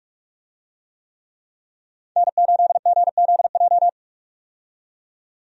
Morse code, heard as a keyed tone, about 42 wpm, 700 Hz. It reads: N8GZJ
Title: USA Call Signs - 5 Characters - 45wpm